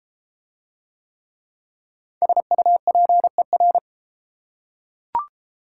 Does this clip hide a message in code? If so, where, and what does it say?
Yes: Morse code from 2.2 to 3.8 s, heard as a keyed tone, about 34 wpm, 690 Hz. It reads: SUPER